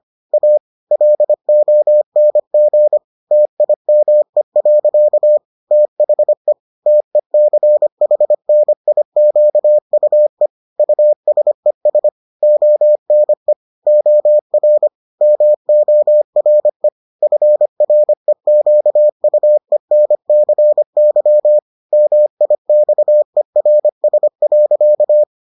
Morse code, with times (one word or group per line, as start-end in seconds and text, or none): 0.3-0.6 A
0.9-3.0 LONG
3.3-5.4 TIME.
5.7-6.5 THE
6.9-10.5 TECHNIQUE
10.8-12.1 USES
12.4-13.5 ONE
13.9-14.9 OR
15.2-16.9 MORE
17.2-21.6 FREQUENCY
21.9-25.2 MIXERS.